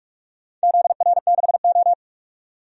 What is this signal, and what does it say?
Morse code, heard as a keyed tone, about 46 wpm, 700 Hz. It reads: ZR6O
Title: DXCC-100 Call Signs - 4 Characters - 45wpm